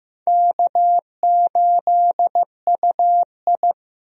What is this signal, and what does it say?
Morse code, heard as a keyed tone, about 16 wpm, 700 Hz. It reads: K8UI
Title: USA Call Signs - 4 Characters - Repeated 3x - 15wpm